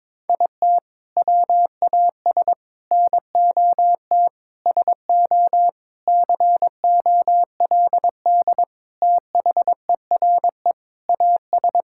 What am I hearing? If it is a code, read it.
Morse code, heard as a keyed tone, about 22 wpm, 700 Hz. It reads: IT WAS NOT SO COLD THERE AS
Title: Sentences from Top 1600 Words 22wpm